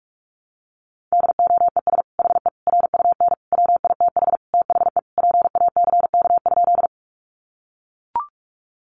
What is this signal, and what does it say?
Morse code, heard as a keyed tone, about 44 wpm, 700 Hz. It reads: DOES HE RUN WITH THE PACK?
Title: Sentences from Top 1300 Words 45wpm